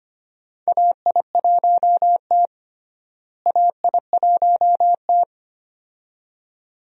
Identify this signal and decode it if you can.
Morse code, heard as a keyed tone, about 25 wpm, 700 Hz. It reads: AI1T AI1T
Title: USA Call Signs - 4 Characters - Repeated 2x - 25wpm